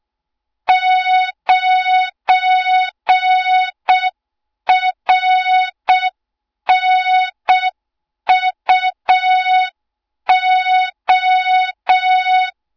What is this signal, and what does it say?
Morse code, heard as a keyed tone, about 6 wpm, 3700 Hz. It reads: 9RNUO
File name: Morze-HA2MN / hang / 20 GYAKORLAT